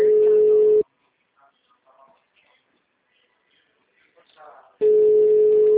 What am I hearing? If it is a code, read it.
Ringback tone (Europe, 425 Hz), with speech.